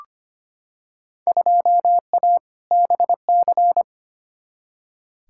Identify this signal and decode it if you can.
Morse code, heard as a keyed tone, about 25 wpm, 700 Hz. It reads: 2A BC
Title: Field Day Exchanges - 25wpm